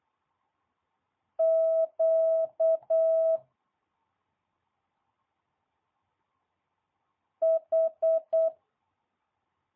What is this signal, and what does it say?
Morse code, heard as a keyed tone, about 8 wpm, 650 Hz. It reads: Q H
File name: jh8cbh / 2022cw